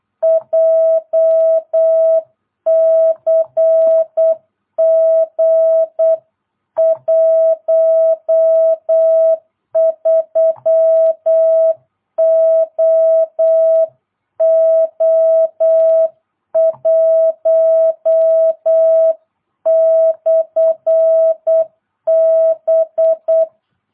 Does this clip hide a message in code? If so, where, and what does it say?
Yes: Morse code from 0.2 to 23.4 s, heard as a keyed tone, about 8 wpm, 650 Hz. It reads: JCG 13OO1/B